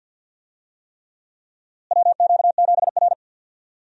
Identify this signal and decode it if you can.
Morse code, heard as a keyed tone, about 52 wpm, 700 Hz. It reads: WQ7R
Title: USA Call Signs - 4 Characters - Repeated 3x - 50wpm